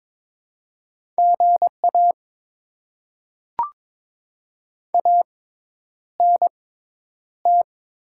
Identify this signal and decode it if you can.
Morse code, heard as a keyed tone, about 22 wpm, spaced as for 10 wpm, 700 Hz. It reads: GA ANT
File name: Top 20 QSO Elements - 6x Character Spacing - 22wpm